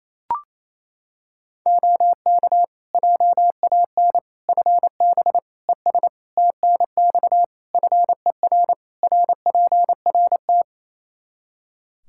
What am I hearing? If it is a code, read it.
Morse code, heard as a keyed tone, about 28 wpm, 700 Hz. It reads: OK JAN FB ES TNX FER RPRT
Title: K1USN SST QSO Practice 28wpm